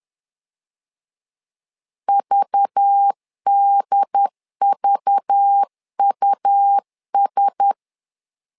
Morse code, heard as a keyed tone, about 10 wpm, 800 Hz. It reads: VDVUS
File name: LEZIONI / 5 LBVC / 5 LBV 10wpm 2248